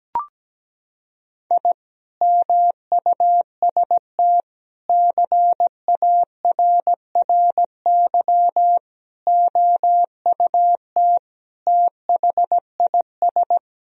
Morse code, heard as a keyed tone, about 17 wpm, 700 Hz. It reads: I MUST CARRY OUT THIS
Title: Sentences from Top 1000 Words - Review Part 2 - 17wpm